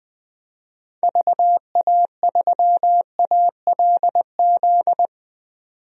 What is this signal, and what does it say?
Morse code, heard as a keyed tone, about 20 wpm, 700 Hz. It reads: VA3ALZ
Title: DXCC-100 Call Signs - 6 Characters - 20wpm